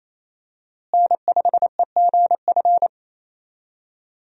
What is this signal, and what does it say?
Morse code, heard as a keyed tone, about 27 wpm, 700 Hz. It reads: N5EGF
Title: USA Call Signs - 5 Characters - Repeated 2x - 28wpm